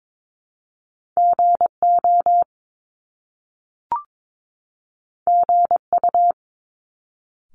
Morse code, heard as a keyed tone, about 22 wpm, 700 Hz. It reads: GO GU
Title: Top Two Letter Combinations in Words 22wpm